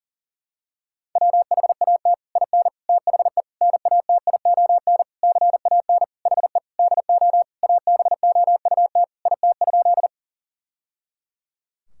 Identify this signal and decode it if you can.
Morse code, heard as a keyed tone, about 40 wpm, 700 Hz. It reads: WHAT IN THE NATION CAN HE DO ABOUT IT?
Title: Sentences from Top 1000 Words - Review Part 6 - 40wpm